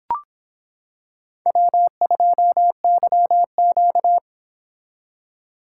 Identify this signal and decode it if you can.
Morse code, heard as a keyed tone, about 26 wpm, 700 Hz. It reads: W2YQ